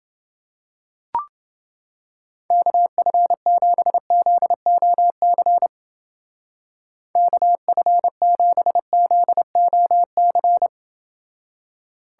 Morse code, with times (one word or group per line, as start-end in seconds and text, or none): 2.5-5.7 KF7ZOC
7.2-10.7 KF7ZOC